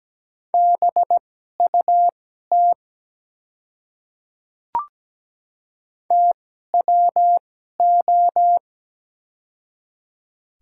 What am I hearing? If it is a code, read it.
Morse code, heard as a keyed tone, about 17 wpm, spaced as for 12 wpm, 700 Hz. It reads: BUT TWO
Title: 2x Character Spacing - Top 200 Words - 17wpm